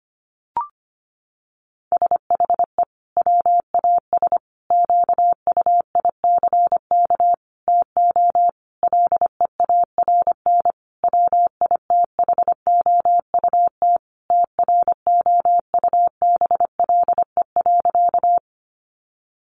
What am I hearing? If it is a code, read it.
Morse code, heard as a keyed tone, about 25 wpm, 700 Hz. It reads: SHE WAS QUICK TO LEARN WITHOUT TROUBLE.
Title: Sentences from Top 1200 Words 25wpm